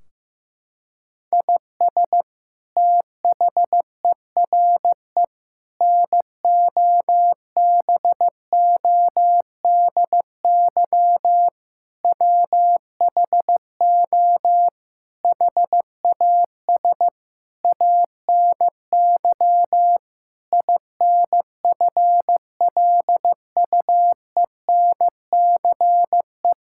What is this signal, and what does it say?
Morse code, heard as a keyed tone, about 15 wpm, 700 Hz. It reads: IS THERE NOBODY WHO HAS ANY INFLUENCE
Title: Sentences from Top 1600 Words 15wpm